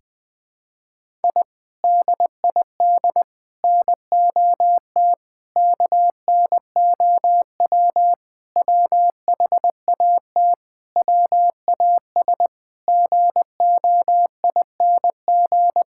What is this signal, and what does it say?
Morse code, heard as a keyed tone, about 21 wpm, 700 Hz. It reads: I DID NOT KNOW WHAT WAS GOING
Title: Sentences with 4-Word Phrases - Spoken Beginning - 20wpm